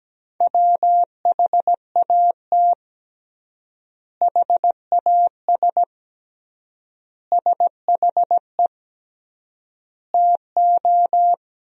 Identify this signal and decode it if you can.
Morse code, heard as a keyed tone, about 17 wpm, 700 Hz. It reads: WHAT HAS SHE TO
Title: 3x Word-Spacing - Sentences from Top 100 Words - 17wpm